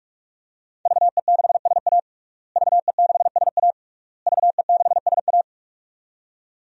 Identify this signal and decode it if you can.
Morse code, heard as a keyed tone, about 46 wpm, 700 Hz. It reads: VE6SA VE6SA VE6SA
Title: DXCC-100 Call Signs - 5 Characters - Repeated 3x - 45wpm